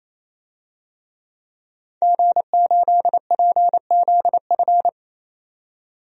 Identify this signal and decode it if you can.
Morse code, heard as a keyed tone, about 28 wpm, 700 Hz. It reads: G8PZF
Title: DXCC-100 Call Signs - 5 Characters - 28wpm